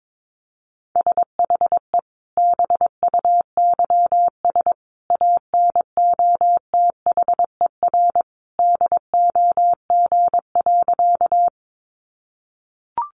Morse code, heard as a keyed tone, about 22 wpm, 700 Hz. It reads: SHE BUYS ANOTHER DOG.
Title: Easy Phrases 22wpm